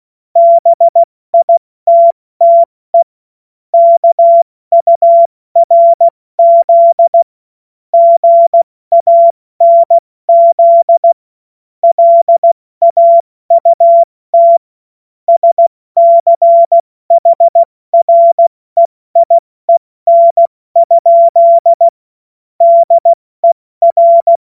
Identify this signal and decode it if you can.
Morse code, heard as a keyed tone, about 16 wpm, 680 Hz. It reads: BITTE KURZ GANZ LAUT SCHREIEN? DER
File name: Morsekurs / Witze / Witz 25. Juli 2023 70 F80